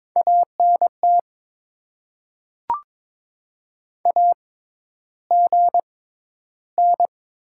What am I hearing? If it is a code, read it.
Morse code, heard as a keyed tone, about 22 wpm, spaced as for 11 wpm, 700 Hz. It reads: ANT AGN
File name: Top 20 QSO Elements - 6x Character Spacing - 22wpm